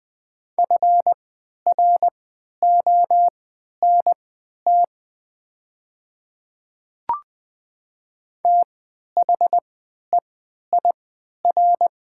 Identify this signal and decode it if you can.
Morse code, heard as a keyed tone, about 20 wpm, spaced as for 11 wpm, 690 Hz. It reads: FRONT THEIR
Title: 3x Character Spacing - Top 500 Words - 20wpm